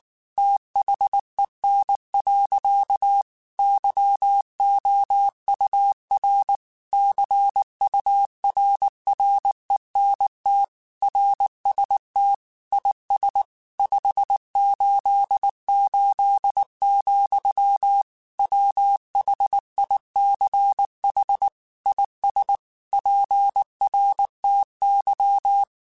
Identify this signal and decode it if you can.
Morse code, heard as a keyed tone, about 19 wpm, 800 Hz. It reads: THEN. YOUR CURRENT RST IS 588, WHICH IS PRTY
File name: QSOs / QSOs-mp3s / WA7QOX 1@18